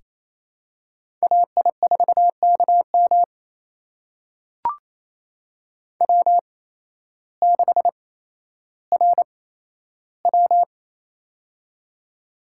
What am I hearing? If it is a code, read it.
Morse code, heard as a keyed tone, about 28 wpm, spaced as for 13 wpm, 700 Hz. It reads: AI4KM W6RW
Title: USA Call Signs - 8x Character Spacing - 28wpm